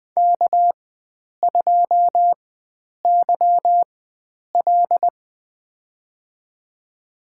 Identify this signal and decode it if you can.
Morse code, heard as a keyed tone, about 20 wpm, spaced as for 9 wpm, 700 Hz. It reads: K2YL
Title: USA Call Signs - 4 Characters - 4x Character Spacing - 20wpm